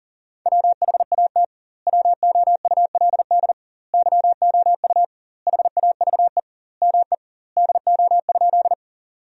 Morse code, heard as a keyed tone, about 40 wpm, 700 Hz. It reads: WHAT WOULD YOU HAVE ME DO?
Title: 2x Word-Spacing - Sentences from Top 200 Words - 40wpm